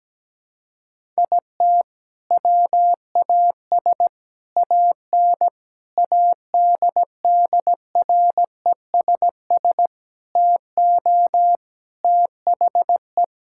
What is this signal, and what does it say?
Morse code, heard as a keyed tone, about 17 wpm, 700 Hz. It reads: IT WAS AN ADDRESS TO THE
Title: Sentences from Top 1000 Words - Review Part 6 - 17wpm